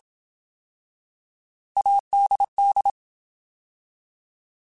Morse code, heard as a keyed tone, about 27 wpm, 800 Hz. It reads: ADD
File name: Alphabetical / Alphabetical mp3s / As @25